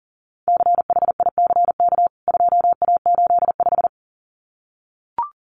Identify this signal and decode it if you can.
Morse code, heard as a keyed tone, about 37 wpm, 700 Hz. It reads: CHICK 2A85
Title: CWT Exchange 40wpm